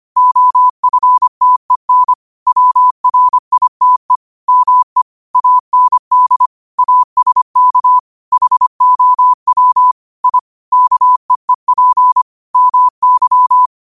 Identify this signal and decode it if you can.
Morse code, heard as a keyed tone, about 25 wpm, 1000 Hz. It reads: OFTEN WRITE ME AND ASK HOW I KEEP MY